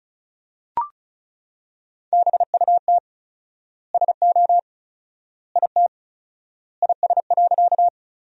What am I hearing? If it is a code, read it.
Morse code, heard as a keyed tone, about 34 wpm, 700 Hz. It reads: BUT SO IT IS.
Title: 4x Word-Spacing - Sentences from Top 100 Words - 35wpm